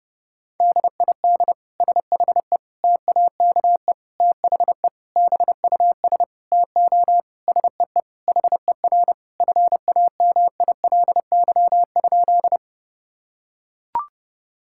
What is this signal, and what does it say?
Morse code, heard as a keyed tone, about 30 wpm, 700 Hz. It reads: DID SHE TAKE THE BUS TO SEE HER FAMILY?
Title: Sentences from Top 1000 Words - Review Part 6 - 30wpm